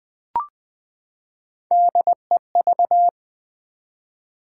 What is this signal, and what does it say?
Morse code, heard as a keyed tone, about 20 wpm, 700 Hz. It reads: DEV
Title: Build A Word - 3 Letters At A Time - 6-Letter Words - 20wpm